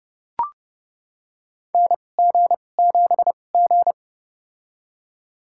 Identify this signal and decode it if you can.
Morse code, heard as a keyed tone, about 30 wpm, spaced as for 22 wpm, 700 Hz. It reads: NG7G
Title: USA Call Signs - 4 Characters - 2x Character Spacing - 30wpm